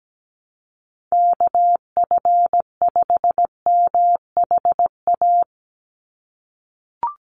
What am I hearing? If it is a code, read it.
Morse code, heard as a keyed tone, about 17 wpm, 700 Hz. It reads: KF5MHA
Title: USA Call Signs 17wpm